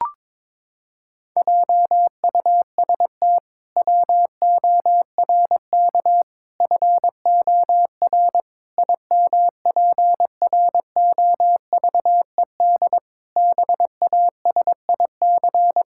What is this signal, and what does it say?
Morse code, heard as a keyed tone, about 22 wpm, 700 Hz. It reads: JUST WORK FOR IMPROVED BASIC
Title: Pangram Challenge v2 - 22wpm